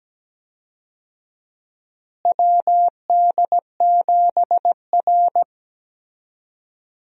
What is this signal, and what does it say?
Morse code, heard as a keyed tone, about 17 wpm, 700 Hz. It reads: WD7R
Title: USA Call Signs - 4 Characters - Repeated 3x - 17wpm